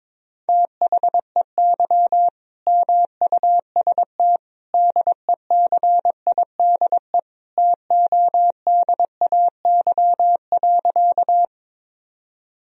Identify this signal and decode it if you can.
Morse code, heard as a keyed tone, about 22 wpm, 700 Hz. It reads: THEY MUST DECIDE TODAY.
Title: Sentences from Top 1000 Words - Review Part 3 - 22wpm